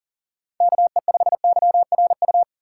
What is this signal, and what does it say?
Morse code, heard as a keyed tone, about 40 wpm, 700 Hz. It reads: KE5YRU
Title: USA Call Signs - 6 Characters - 4x Character Spacing - 40wpm